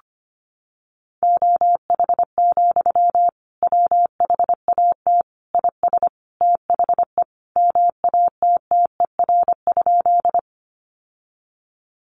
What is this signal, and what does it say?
Morse code, heard as a keyed tone, about 25 wpm, 700 Hz. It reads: OH, WHAT IS THE MATTER?
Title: Sentences from Top 700 Words 25wpm